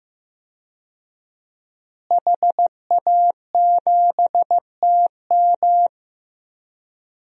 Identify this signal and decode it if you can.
Morse code, heard as a keyed tone, about 16 wpm, 700 Hz. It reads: HA7TM
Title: DXCC-100 Call Signs - 5 Characters - Repeated 2x - 15wpm